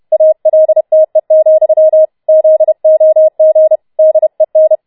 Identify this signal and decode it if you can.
Morse code, heard as a keyed tone, about 31 wpm, 630 Hz. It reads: ALTE, ZOG DEN
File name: Schneewittchen Teil4 32wpm